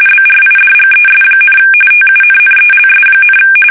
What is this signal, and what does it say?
RTTY, about 45.45 baud. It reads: BRIDGES, TUNNELS,
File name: fsk850 / battleshipnj